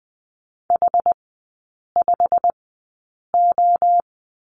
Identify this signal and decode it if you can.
Morse code, heard as a keyed tone, about 19 wpm, spaced as for 8 wpm, 700 Hz. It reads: H5O
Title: 121 Lesson 20e - Number 3 - Three Characters 20wpm